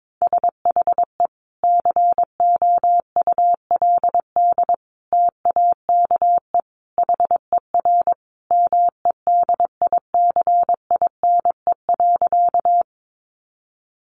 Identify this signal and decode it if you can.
Morse code, heard as a keyed tone, about 22 wpm, 700 Hz. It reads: SHE COULD TAKE HER MEDICINE.
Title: Sentences from Top 900 Words 22wpm